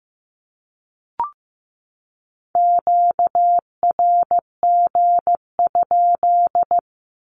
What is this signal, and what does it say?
Morse code, heard as a keyed tone, about 15 wpm, 700 Hz. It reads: QRG?
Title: Q-Codes 15wpm